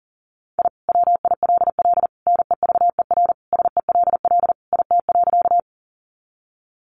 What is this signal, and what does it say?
Morse code, heard as a keyed tone, about 40 wpm, 700 Hz. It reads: I WILL NEVER SELL IT.
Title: Sentences from Top 1100 Words 40wpm